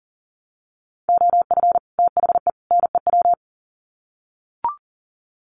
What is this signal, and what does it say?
Morse code, heard as a keyed tone, about 40 wpm, 700 Hz. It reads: OF THE NEW
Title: Sets of 3 Words 40wpm